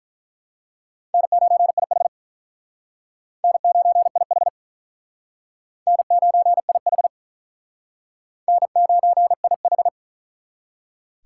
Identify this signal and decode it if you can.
Morse code, heard as a keyed tone, about 44 wpm, 700 Hz. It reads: N9IH N9IH N9IH N9IH